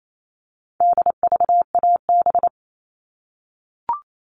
Morse code, heard as a keyed tone, about 28 wpm, 700 Hz. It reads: DVAB
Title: All Cut Numbers - 4-Digit 28wpm